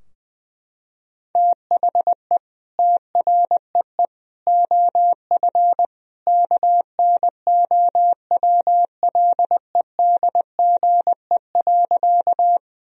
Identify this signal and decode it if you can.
Morse code, heard as a keyed tone, about 21 wpm, 700 Hz. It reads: THE TREE OF KNOWLEDGE.
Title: Sentences from Top 1000 Words - Review Part 3 - 20wpm